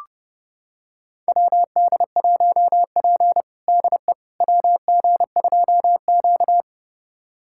Morse code, heard as a keyed tone, about 30 wpm, 700 Hz. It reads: WD1P DE WG2Q
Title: USA Callsign DE Callsign - 4 Characters - 30wpm